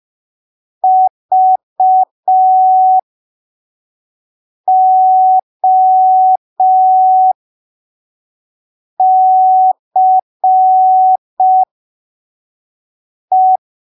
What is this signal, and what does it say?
Morse code, heard as a keyed tone, about 5 wpm, spaced as for 3 wpm, 750 Hz. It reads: VOCE
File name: arquivos / texto / texto 17 05ppm